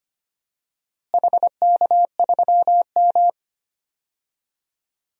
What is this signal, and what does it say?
Morse code, heard as a keyed tone, about 25 wpm, 700 Hz. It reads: HK3M
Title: DXCC-100 Call Signs - 4 Characters - Repeated 2x - 25wpm